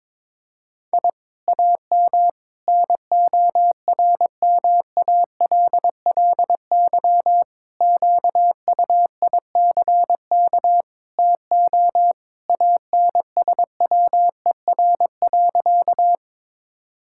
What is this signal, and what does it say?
Morse code, heard as a keyed tone, about 22 wpm, 700 Hz. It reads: I AM NORMALLY QUICK TO ANSWER.
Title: Sentences from Top 1000 Words - Review Part 6 - 22wpm